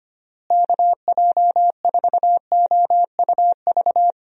Morse code, heard as a keyed tone, about 25 wpm, 700 Hz. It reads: KJ4OUV
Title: USA Call Signs - 6 Characters - Repeated 3x - 25wpm